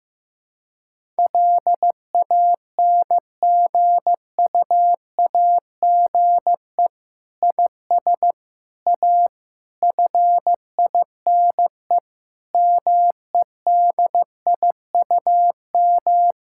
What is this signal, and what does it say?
Morse code, heard as a keyed tone, about 15 wpm, 700 Hz. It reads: LANGUAGE IS A FINE MEDIUM
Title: Sentences from Top 1000 Words - Review Part 6 - 15wpm